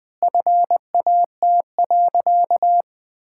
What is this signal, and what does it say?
Morse code, heard as a keyed tone, about 21 wpm, 700 Hz. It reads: FAT.
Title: Sentences from Top 1000 Words - Review Part 5 - 20wpm